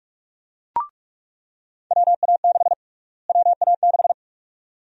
Morse code, heard as a keyed tone, about 46 wpm, 700 Hz. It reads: WA6 WA6
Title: USA Call Sign Prefixes - Repeated 2x - 45wpm